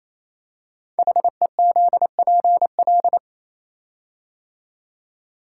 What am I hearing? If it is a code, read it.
Morse code, heard as a keyed tone, about 28 wpm, 700 Hz. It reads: HEZPL
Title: Code Groups - 5 Letters - 28wpm